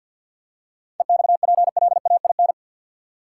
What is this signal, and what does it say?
Morse code, heard as a keyed tone, about 53 wpm, 700 Hz. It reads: EXPLAIN
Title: Seven Letter Words - 50wpm